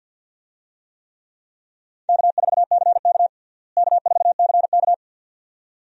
Morse code, heard as a keyed tone, about 52 wpm, 700 Hz. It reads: K4CK K4CK
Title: USA Call Signs - 4 Characters - Repeated 2x - 50wpm